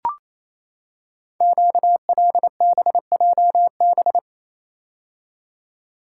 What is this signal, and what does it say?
Morse code, heard as a keyed tone, about 29 wpm, 690 Hz. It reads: QLBJB
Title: Code Groups - 5 Letters - 28wpm